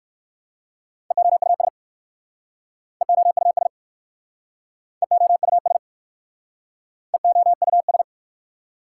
Morse code, heard as a keyed tone, about 60 wpm, 700 Hz. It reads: EOUS EOUS EOUS EOUS